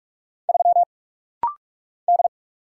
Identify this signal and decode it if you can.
Morse code, heard as a keyed tone, about 46 wpm, 700 Hz. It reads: 3 D
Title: Single Letter-Number Alt - Rapid-Fire 45wpm